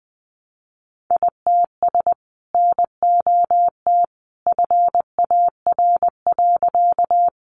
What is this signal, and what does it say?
Morse code, heard as a keyed tone, about 20 wpm, 700 Hz. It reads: ITS NOT FAR.
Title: Sentences from Top 600 Words 20wpm